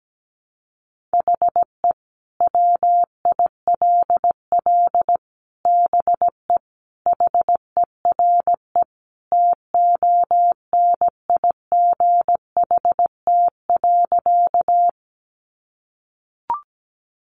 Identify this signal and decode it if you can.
Morse code, heard as a keyed tone, about 17 wpm, 700 Hz. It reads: HE WILL BE HERE TONIGHT.
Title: Sentences from Top 600 Words 17wpm